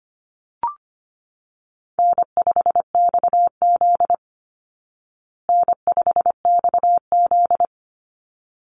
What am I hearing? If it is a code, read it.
Morse code, heard as a keyed tone, about 25 wpm, 700 Hz. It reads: N5XZ N5XZ
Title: Easy Call Signs Repeated 2x - 25wpm